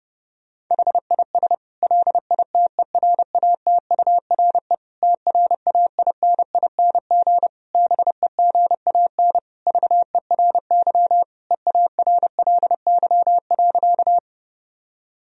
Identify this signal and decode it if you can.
Morse code, heard as a keyed tone, about 32 wpm, 700 Hz. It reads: HIS LITERATURE TRAINING BEGAN VERY EARLY.
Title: Sentences from Top 1000 Words - Review Part 4 - 30wpm